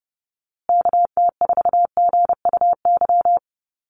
Morse code, heard as a keyed tone, about 30 wpm, 700 Hz. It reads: KT4GUY
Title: USA Call Signs Repeated 2x 30wpm